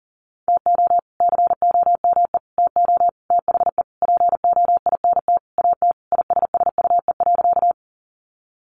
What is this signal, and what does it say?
Morse code, heard as a keyed tone, about 40 wpm, 700 Hz. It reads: TO COME TO THE POINT AT ISSUE.
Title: Sentences from Top 900 Words 40wpm